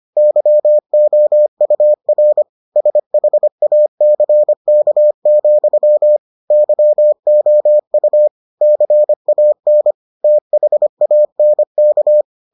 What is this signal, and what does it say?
Morse code, heard as a keyed tone, about 25 wpm, 600 Hz. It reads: YOUR SHACK, YOU CAN THANK